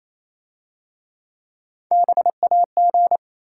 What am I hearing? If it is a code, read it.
Morse code, heard as a keyed tone, about 27 wpm, 700 Hz. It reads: BAG
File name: Easy Words - Speed-Racing with Hints - 22wpm